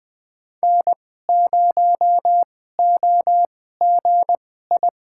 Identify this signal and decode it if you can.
Morse code, heard as a keyed tone, about 20 wpm, spaced as for 14 wpm, 700 Hz. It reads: N0OGI
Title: USA Call Signs - 5 Characters - 2x Character Spacing - 20wpm